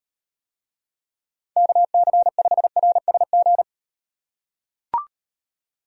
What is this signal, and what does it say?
Morse code, heard as a keyed tone, about 34 wpm, 700 Hz. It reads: KC5RSG